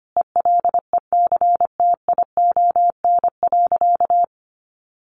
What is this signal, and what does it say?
Morse code, heard as a keyed tone, about 25 wpm, 700 Hz. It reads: ELECTION.
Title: Sentences from Top 1500 Words 25wpm